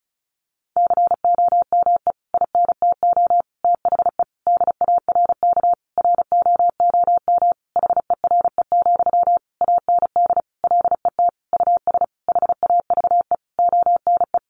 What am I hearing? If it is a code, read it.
Morse code, heard as a keyed tone, about 34 wpm, 700 Hz. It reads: COME INTO THE DARK ROOM HERE, AND LET US HAVE ONE
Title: Sentences from Top 1000 Words 35wpm